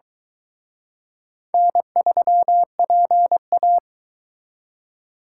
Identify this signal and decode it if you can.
Morse code, heard as a keyed tone, about 23 wpm, 700 Hz. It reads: N3PA